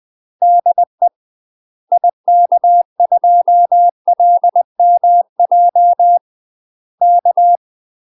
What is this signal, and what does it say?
Morse code, heard as a keyed tone, about 20 wpm, 700 Hz. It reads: DE IK2LMJ K
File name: UK MASTER QSOs TRAINING / 20WPM / UK MASTER QSOs 0045